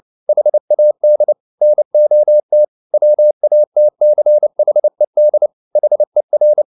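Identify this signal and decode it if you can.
Morse code, heard as a keyed tone, about 29 wpm, 600 Hz. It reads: HAD NOT WATCHED HER
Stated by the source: Morse code at 29 wpm, sending Ken Follett's novel Never in English, from CW med Gnister.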